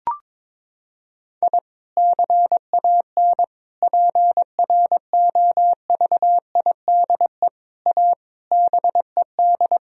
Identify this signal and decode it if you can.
Morse code, heard as a keyed tone, about 22 wpm, 700 Hz. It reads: I CAN PROVIDE A BED
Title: Sentences from Top 1000 Words - Review Part 3 - 22wpm